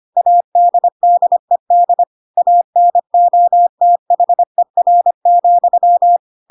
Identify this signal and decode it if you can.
Morse code, heard as a keyed tone, about 25 wpm, 700 Hz. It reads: ADDED ANOTHER,
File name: Pinocchio / Pinocchio ch3 25 25 ik1hge